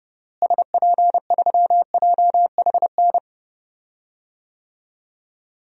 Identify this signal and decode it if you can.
Morse code, heard as a keyed tone, about 31 wpm, 700 Hz. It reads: SP3JHN
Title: DXCC-100 Call Signs - 6 Characters - Repeated 2x - 30wpm